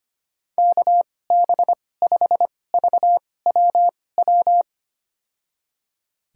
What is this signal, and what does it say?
Morse code, heard as a keyed tone, about 25 wpm, spaced as for 18 wpm, 700 Hz. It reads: KB5VWW
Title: USA Call Signs - 2x Character Spacing - 25wpm